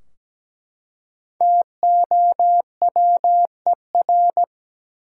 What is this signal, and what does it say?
Morse code, heard as a keyed tone, about 17 wpm, 700 Hz. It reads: TOWER
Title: Five Letter Words - 17wpm